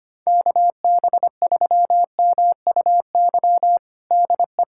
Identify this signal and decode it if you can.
Morse code, heard as a keyed tone, about 25 wpm, 700 Hz. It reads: KB3MUY DE
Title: USA Callsign DE Callsign - 6 Characters - 25wpm